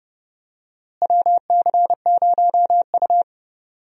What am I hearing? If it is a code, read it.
Morse code, heard as a keyed tone, about 30 wpm, 700 Hz. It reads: WC0U
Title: USA Call Signs - 4 Characters - Repeated 2x - 30wpm